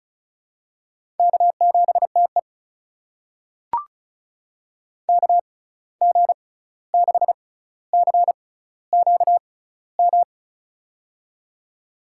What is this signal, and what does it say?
Morse code, heard as a keyed tone, about 34 wpm, spaced as for 16 wpm, 700 Hz. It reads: K7TE KG6CQM
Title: USA Call Signs - 6x Character Spacing - 35wpm